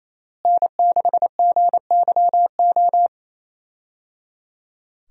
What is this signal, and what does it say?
Morse code, heard as a keyed tone, about 28 wpm, 700 Hz. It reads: N6GYO
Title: USA Call Signs - 5 Characters - Repeated 3x - 28wpm